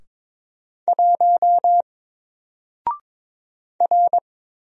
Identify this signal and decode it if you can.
Morse code, heard as a keyed tone, about 22 wpm, 700 Hz. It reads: 1 R
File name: Single-Letter-Number-22wpm